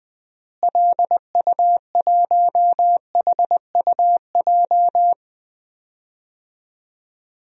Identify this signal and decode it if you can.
Morse code, heard as a keyed tone, about 20 wpm, 700 Hz. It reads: LU1HUJ
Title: DXCC-100 Call Signs - 6 Characters - Repeated 3x - 20wpm